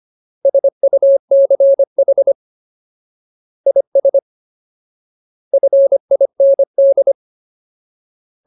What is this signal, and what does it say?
Morse code, heard as a keyed tone, about 25 wpm, 550 Hz. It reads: SUCH IS FIND
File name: Ultimate Word List / 25WPM Ultimate